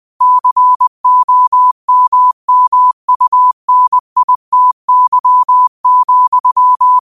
Morse code, heard as a keyed tone, about 20 wpm, 1000 Hz. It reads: COMMUNITY,
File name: qotdfiles / 20wpm